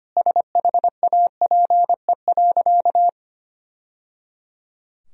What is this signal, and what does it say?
Morse code, heard as a keyed tone, about 25 wpm, 700 Hz. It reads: SHAPE.
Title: Sentences from Top 1000 Words - Review Part 3 - 25wpm